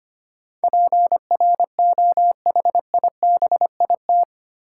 Morse code, heard as a keyed tone, about 25 wpm, 700 Hz. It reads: PROHIBIT
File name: 8 Letter Words - First 3 Letters Given - 25wpm